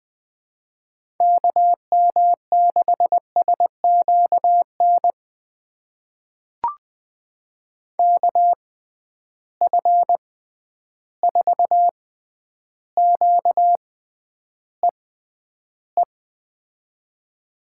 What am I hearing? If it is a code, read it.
Morse code, heard as a keyed tone, about 20 wpm, spaced as for 10 wpm, 700 Hz. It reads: KM6SQN KF4QEE
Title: USA Call Signs - 6x Character Spacing - 20wpm